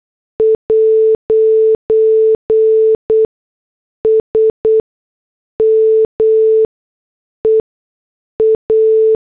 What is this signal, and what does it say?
Morse code, heard as a keyed tone, about 8 wpm, spaced as for 6 wpm, 440 Hz. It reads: 'S MEA
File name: morse / headlines